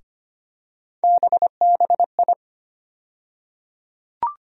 Morse code, heard as a keyed tone, about 25 wpm, 700 Hz. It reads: BBI
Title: USA Call Sign Postfixes - 3 Letters 25wpm